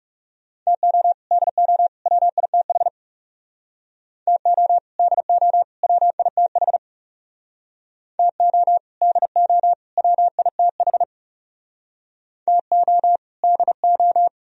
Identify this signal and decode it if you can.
Morse code, heard as a keyed tone, about 36 wpm, 700 Hz. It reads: TO DO WITH TO DO WITH TO DO WITH TO DO
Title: Speed-Racing - Sets of 3 Words 30wpm